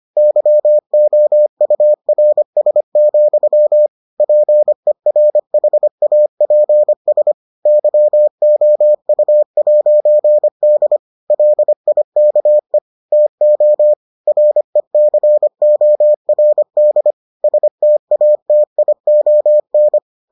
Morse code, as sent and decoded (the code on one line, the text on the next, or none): -.-- --- ..- .-. ... --..-- / .--. .-. .... .- .--. ... / -.-- --- ..- .----. -.. / .-.. .. -.- . / - --- / .-. -.-. --- .-. -.. / ... - .- - .. --- -.
YOURS, PERHAPS YOU'D LIKE TO RECORD STATION